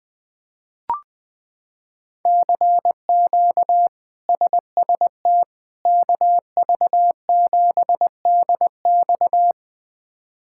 Morse code, heard as a keyed tone, about 20 wpm, 700 Hz. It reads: CQ SST KV7DX
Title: K1USN SST QSO Practice - 20wpm